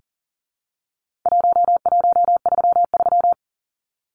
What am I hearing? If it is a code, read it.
Morse code, heard as a keyed tone, about 40 wpm, 700 Hz. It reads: 1133
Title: Numbers - 4-Digit 40wpm